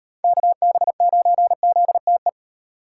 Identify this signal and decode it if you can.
Morse code, heard as a keyed tone, about 35 wpm, 700 Hz. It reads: KB9ZTE